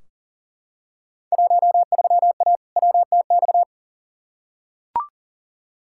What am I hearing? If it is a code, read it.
Morse code, heard as a keyed tone, about 40 wpm, 700 Hz. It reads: 13A WTX
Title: Field Day Exchanges Encore - 40wpm